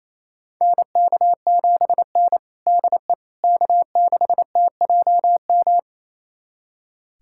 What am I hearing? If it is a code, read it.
Morse code, heard as a keyed tone, about 28 wpm, 700 Hz. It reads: NK7N DE K6TJM
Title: USA Callsign DE Callsign 28wpm